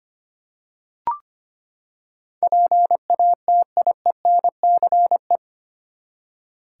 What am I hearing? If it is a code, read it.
Morse code, heard as a keyed tone, about 25 wpm, 700 Hz. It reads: PATIENCE
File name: Eight Letter Words - 25wpm